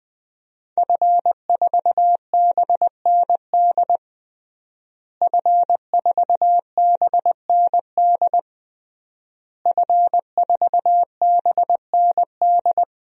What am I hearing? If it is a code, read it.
Morse code, heard as a keyed tone, about 21 wpm, 700 Hz. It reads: F4BND F4BND F4BND
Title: DXCC-100 Call Signs - 5 Characters - Repeated 3x - 20wpm